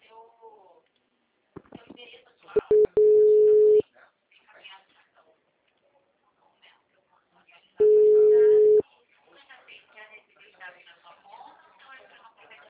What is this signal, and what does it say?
Ringback tone (Europe, 425 Hz), with speech.